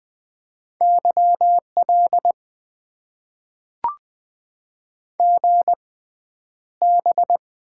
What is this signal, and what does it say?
Morse code, heard as a keyed tone, about 20 wpm, spaced as for 10 wpm, 700 Hz. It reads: YL GB
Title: Top 90 QSO Elements - 6x Character Spacing - 20wpm